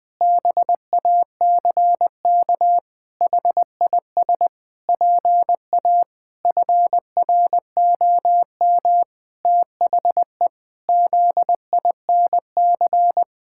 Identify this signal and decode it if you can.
Morse code, heard as a keyed tone, about 20 wpm, 700 Hz. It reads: BACK HIS PA FROM THE ZINC
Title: Pangram Challenge v2 - 20wpm